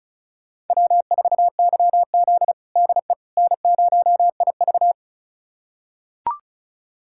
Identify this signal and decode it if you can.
Morse code, heard as a keyed tone, about 36 wpm, 700 Hz. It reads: W4YZ DE N0IV
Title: USA Callsign DE Callsign - 4 Characters - 35wpm